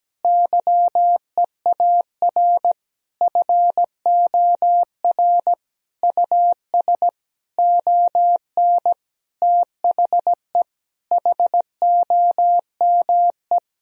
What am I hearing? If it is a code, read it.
Morse code, heard as a keyed tone, about 17 wpm, 700 Hz. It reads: YEAR FOR US ON THE HOME
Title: Sentences from Top 1000 Words - Review Part 2 - 17wpm